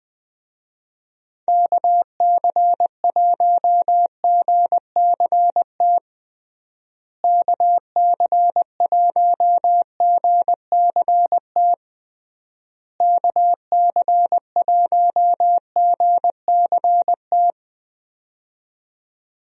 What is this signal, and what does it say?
Morse code, heard as a keyed tone, about 20 wpm, 700 Hz. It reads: KC1GCT KC1GCT KC1GCT
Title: USA Call Signs - 6 Characters - Repeated 3x - 20wpm